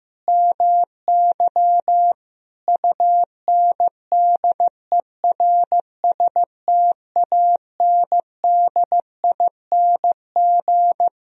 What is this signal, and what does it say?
Morse code, heard as a keyed tone, about 15 wpm, 700 Hz. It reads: MY UNDERSTANDING
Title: Sentences from Top 1000 Words - Review Part 3 - 15wpm